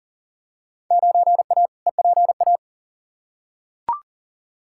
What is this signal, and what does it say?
Morse code, heard as a keyed tone, about 37 wpm, 700 Hz. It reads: 9A EPA
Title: Field Day Exchanges Encore - 40wpm